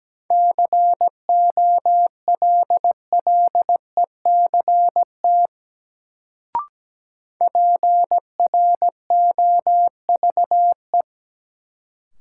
Morse code, heard as a keyed tone, about 17 wpm, 700 Hz. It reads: COLLECT PROVE
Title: Sentences from Top 1600 Words - New Word Drill Alt - 17wpm